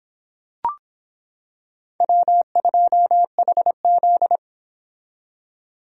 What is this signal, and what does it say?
Morse code, heard as a keyed tone, about 26 wpm, 700 Hz. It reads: W2HZ